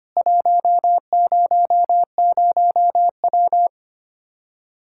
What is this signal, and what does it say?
Morse code, heard as a keyed tone, about 25 wpm, 700 Hz. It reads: 100W
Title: Practice QSO - Exchange 2 - Info - 25wpm